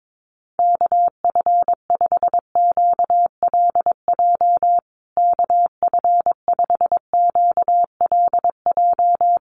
Morse code, heard as a keyed tone, about 22 wpm, 700 Hz. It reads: KF5QLJ KF5QLJ
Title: USA Call Signs Repeated 3x 22wpm